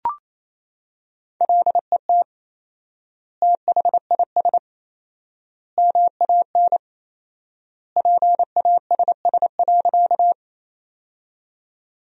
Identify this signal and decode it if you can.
Morse code, heard as a keyed tone, about 27 wpm, 700 Hz. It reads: LET THIS MAN PASS.
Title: 4x Word-Spacing - Sentences from Top 400 Words - 28wpm